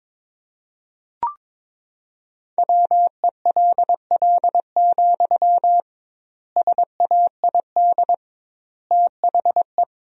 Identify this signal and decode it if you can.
Morse code, heard as a keyed tone, about 22 wpm, 700 Hz. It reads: WELL, SAID THE
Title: 2x Word-Spacing - Sentences from Top 400 Words - 22wpm